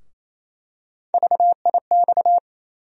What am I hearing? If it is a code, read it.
Morse code, heard as a keyed tone, about 28 wpm, 700 Hz. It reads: VIX
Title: USA Call Sign Postfixes - 3 Letters 28wpm